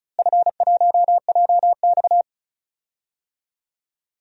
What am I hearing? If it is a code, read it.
Morse code, heard as a keyed tone, about 34 wpm, 700 Hz. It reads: F1JX
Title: DXCC-100 Call Signs - 4 Characters - Repeated 3x - 35wpm